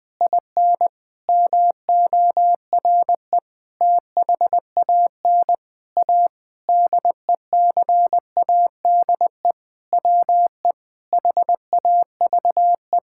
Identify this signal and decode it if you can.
Morse code, heard as a keyed tone, about 20 wpm, 700 Hz. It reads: IN MORE THAN A DECADE WE HAVE
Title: Sentences with 9-Word Phrases - Spoken Beginning - 20wpm